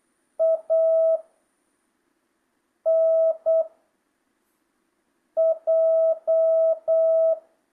Morse code, heard as a keyed tone, about 8 wpm, 1300 Hz. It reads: A N J
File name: jh8cbh / 2025cw